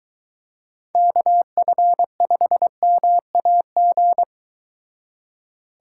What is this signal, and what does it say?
Morse code, heard as a keyed tone, about 23 wpm, 700 Hz. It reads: KF5MAG